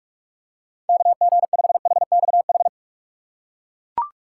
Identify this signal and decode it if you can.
Morse code, heard as a keyed tone, about 41 wpm, 700 Hz. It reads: KG5HXH
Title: USA Call Signs - 6 Characters - 4x Character Spacing - 45wpm